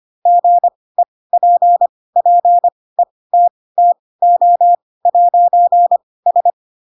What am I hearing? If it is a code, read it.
Morse code, heard as a keyed tone, about 25 wpm, spaced as for 18 wpm, 700 Hz. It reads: GEPPETTO'S
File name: Pinocchio / Pinocchio ch2 12 25 ik1hge